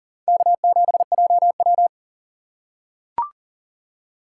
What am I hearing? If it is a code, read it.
Morse code, heard as a keyed tone, about 40 wpm, 700 Hz. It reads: K7JW
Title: USA Call Signs - 4 Characters - 2x Character Spacing - 40wpm